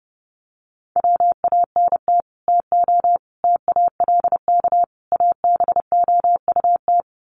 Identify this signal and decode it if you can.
Morse code, heard as a keyed tone, about 30 wpm, 700 Hz. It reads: WANT TO TALK ABOUT
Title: Sets of 4 Words 30wpm